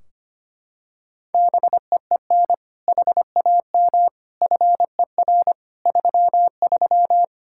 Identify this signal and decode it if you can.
Morse code, heard as a keyed tone, about 25 wpm, 700 Hz. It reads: BEEN HAM FER 33
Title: Practice QSO - Exchange 3 - More Info - 25wpm